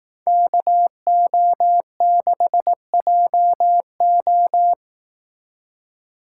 Morse code, heard as a keyed tone, about 18 wpm, 700 Hz. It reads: KO6JO
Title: Speed Racing - USA 2x2 Call Signs 15wpm